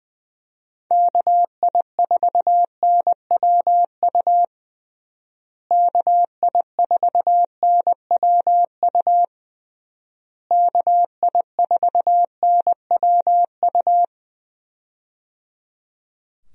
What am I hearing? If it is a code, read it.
Morse code, heard as a keyed tone, about 20 wpm, 700 Hz. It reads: KI4NWU KI4NWU KI4NWU
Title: USA Call Signs - 6 Characters - Repeated 3x - 20wpm